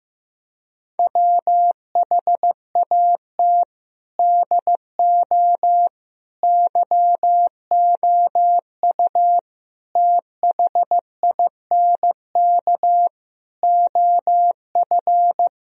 Morse code, heard as a keyed tone, about 15 wpm, 700 Hz. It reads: WHAT DO YOU THINK OF
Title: Speed-Racing - Sets of 5 Words 15wpm